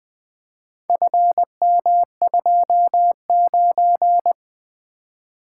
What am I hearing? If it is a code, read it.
Morse code, heard as a keyed tone, about 20 wpm, 700 Hz. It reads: FM29
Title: Four Letter-Number Combinations - 20wpm